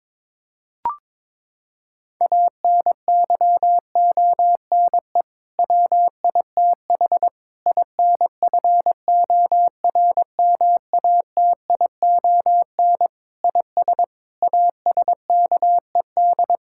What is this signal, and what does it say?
Morse code, heard as a keyed tone, about 21 wpm, 700 Hz. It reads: ANYONE WITH INFORMATION IS ASKED
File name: practice / 22wpm / Sets of 6 Words - 22wpm